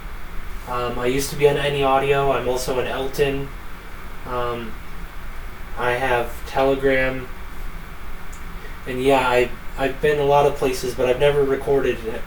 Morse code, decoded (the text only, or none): none